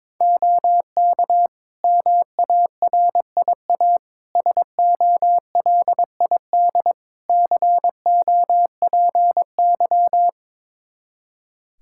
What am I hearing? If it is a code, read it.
Morse code, heard as a keyed tone, about 21 wpm, 700 Hz. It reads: OK MARIA SOLID COPY
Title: Practice QSO - Exchange 3 - More Info - 22wpm